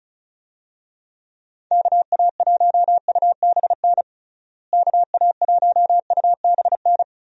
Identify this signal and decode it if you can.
Morse code, heard as a keyed tone, about 34 wpm, 700 Hz. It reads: KA1UBN KA1UBN
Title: USA Call Signs - 6 Characters - Repeated 2x - 35wpm